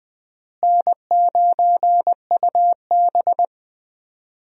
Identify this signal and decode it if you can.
Morse code, heard as a keyed tone, about 20 wpm, 700 Hz. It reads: N9UB
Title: USA Call Signs - 4 Characters - Repeated 3x - 20wpm